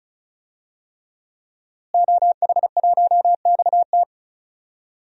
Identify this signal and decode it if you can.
Morse code, heard as a keyed tone, about 36 wpm, 700 Hz. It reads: OH1XT
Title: DXCC-100 Call Signs - 5 Characters - Repeated 3x - 35wpm